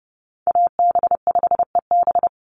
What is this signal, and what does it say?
Morse code, heard as a keyed tone, about 30 wpm, 700 Hz. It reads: AB5EB
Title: Call Signs - 30wpm